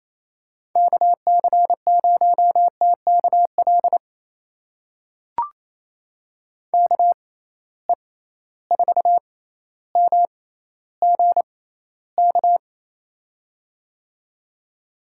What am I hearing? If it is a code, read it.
Morse code, heard as a keyed tone, about 28 wpm, spaced as for 14 wpm, 700 Hz. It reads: KC0TKL KE4MGK
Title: USA Call Signs - 6 Characters - 6x Character Spacing - 28wpm